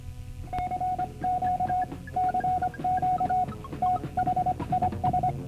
Morse code, heard as a keyed tone, about 25 wpm, 730 Hz. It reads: CQ CQ THIS